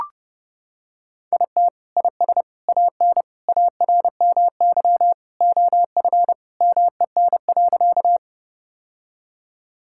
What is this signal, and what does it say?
Morse code, heard as a keyed tone, about 30 wpm, 700 Hz. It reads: IT IS AN ARMY OF MEN.
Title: Sentences from Top 1000 Words - Review Part 5 - 30wpm